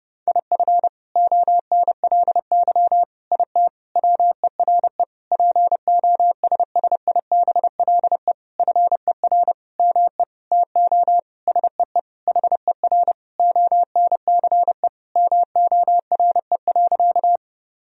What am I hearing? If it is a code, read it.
Morse code, heard as a keyed tone, about 30 wpm, 700 Hz. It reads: IF ONLY IT WERE POSSIBLE FER ME TO SEE HER ONCE MORE.
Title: Sentences from Top 500 Words and Abbreviations 30wpm